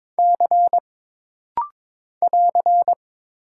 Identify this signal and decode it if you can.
Morse code, heard as a keyed tone, about 22 wpm, 700 Hz. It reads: C <AR>
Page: This rapid-fire timing is geared to push your ICR (Instant Character Recognition) skills.